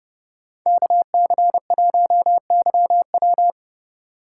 Morse code, heard as a keyed tone, about 31 wpm, 700 Hz. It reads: KC1YW